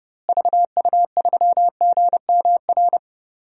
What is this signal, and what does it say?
Morse code, heard as a keyed tone, about 30 wpm, 700 Hz. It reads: VU3GMR
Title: DXCC-100 Call Signs - 6 Characters - Repeated 2x - 30wpm